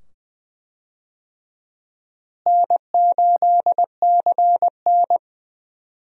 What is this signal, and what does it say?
Morse code, heard as a keyed tone, about 20 wpm, 700 Hz. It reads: N8CN
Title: USA Call Signs - 4 Characters - Repeated 2x - 20wpm